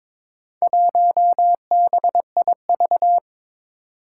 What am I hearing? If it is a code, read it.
Morse code, heard as a keyed tone, about 22 wpm, 700 Hz. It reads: 1BIV
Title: Four Challenging Letter-Number Combinations 22wpm